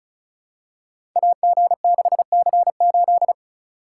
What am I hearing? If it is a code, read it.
Morse code, heard as a keyed tone, about 36 wpm, 690 Hz. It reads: AG6C8
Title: Code Groups - 5 Letter:Numbers - 35wpm